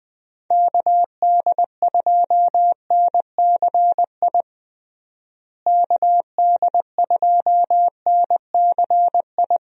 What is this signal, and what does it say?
Morse code, heard as a keyed tone, about 20 wpm, 700 Hz. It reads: KD2NCI KD2NCI
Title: USA Call Signs - 6 Characters - Repeated 2x - 20wpm